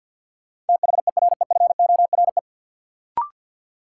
Morse code, heard as a keyed tone, about 46 wpm, 700 Hz. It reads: THEREFORE
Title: Sentences from Top 800 Words - New Word Drill - 50wpm